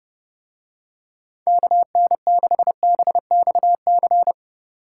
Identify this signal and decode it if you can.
Morse code, heard as a keyed tone, about 27 wpm, 700 Hz. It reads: KN6BXC